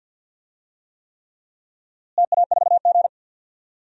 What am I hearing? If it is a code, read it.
Morse code, heard as a keyed tone, about 53 wpm, 700 Hz. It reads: TA4G
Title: DXCC-100 Call Signs - 4 Characters - Repeated 3x - 50wpm